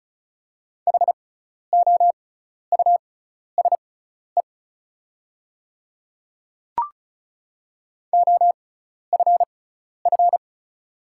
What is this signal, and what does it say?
Morse code, heard as a keyed tone, about 35 wpm, spaced as for 12 wpm, 690 Hz. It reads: HOUSE OFF
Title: 6x Character Spacing - Top 200 Words - 35wpm